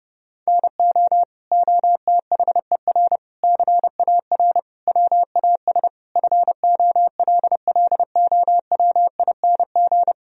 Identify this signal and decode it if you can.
Morse code, heard as a keyed tone, about 30 wpm, 700 Hz. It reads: NO OTHER CAR WAS FOLLOWING
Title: Sentences from Top 1000 Words - Review Part 4 - 30wpm